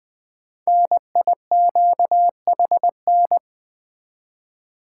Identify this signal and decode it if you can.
Morse code, heard as a keyed tone, about 21 wpm, 700 Hz. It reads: NIQHN
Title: Code Groups - 5 Letter:Numbers - 20wpm